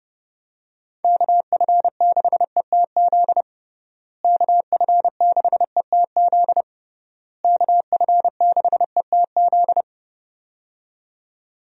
Morse code, heard as a keyed tone, about 32 wpm, 700 Hz. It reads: KF6ETZ KF6ETZ KF6ETZ
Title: USA Call Signs - 6 Characters - Repeated 3x - 30wpm